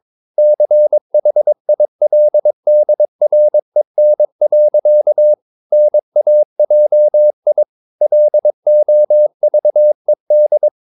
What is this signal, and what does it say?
Morse code, heard as a keyed tone, about 22 wpm, 600 Hz. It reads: CHILDREN. NAJI LOVED
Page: Romanen Never af Ken Follet på engelsk.